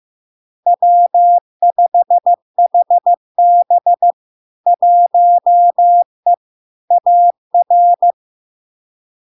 Morse code, heard as a keyed tone, about 15 wpm, 700 Hz. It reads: W5HB 1E AR
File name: LARK-FD-2022-Log-in-CW-15wpm-700-Hz